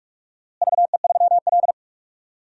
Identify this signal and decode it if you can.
Morse code, heard as a keyed tone, about 46 wpm, 700 Hz. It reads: VE3L
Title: DXCC-100 Call Signs - 4 Characters - Repeated 3x - 45wpm